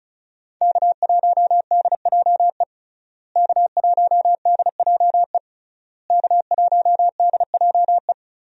Morse code, heard as a keyed tone, about 36 wpm, 700 Hz. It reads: K1DJE K1DJE K1DJE
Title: USA Call Signs - 5 Characters - Repeated 3x - 35wpm